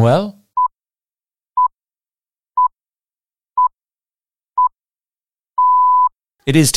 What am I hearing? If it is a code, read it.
Radio time signal (time pips), with speech. It is heard from 0.6 to 6.1 s.